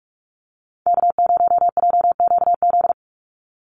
Morse code, heard as a keyed tone, about 46 wpm, 700 Hz. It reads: K0JQZ
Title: NA SOTA Chaser Callsigns 45wpm